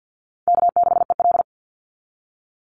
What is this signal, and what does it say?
Morse code, heard as a keyed tone, about 49 wpm, 700 Hz. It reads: K6EL
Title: NA SOTA Chaser Callsigns 50wpm